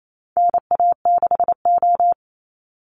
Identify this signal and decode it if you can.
Morse code, heard as a keyed tone, about 28 wpm, 700 Hz. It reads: NA6O
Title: Call Signs - 28wpm